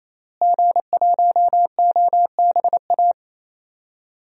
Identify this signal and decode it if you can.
Morse code, heard as a keyed tone, about 28 wpm, 700 Hz. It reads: G1OBA
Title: DXCC-100 Call Signs - 5 Characters - Repeated 3x - 28wpm